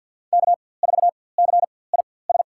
Morse code, heard as a keyed tone, about 50 wpm, spaced as for 23 wpm, 700 Hz. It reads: K4CIS
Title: USA Call Signs - 5 Characters - 4x Character Spacing - 50wpm